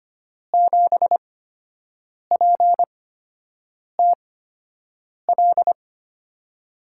Morse code, heard as a keyed tone, about 25 wpm, spaced as for 7 wpm, 700 Hz. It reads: 7PTL
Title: USA Call Signs - 8x Character Spacing - 25wpm